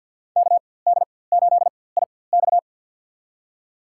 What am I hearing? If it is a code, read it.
Morse code, heard as a keyed tone, about 53 wpm, spaced as for 24 wpm, 700 Hz. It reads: KD8IX
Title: USA Call Signs - 4x Character Spacing - 50wpm